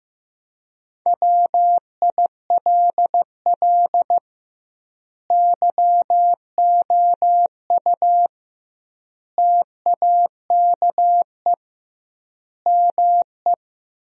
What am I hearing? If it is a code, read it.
Morse code, heard as a keyed tone, about 15 wpm, 700 Hz. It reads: WILL YOU TAKE ME
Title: 2x Word-Spacing - Sentences from Top 200 Words - 15wpm